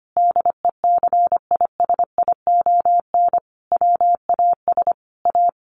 Morse code, heard as a keyed tone, about 25 wpm, 700 Hz. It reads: DECISION WAS A
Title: Sentences from Top 1400 Words 25wpm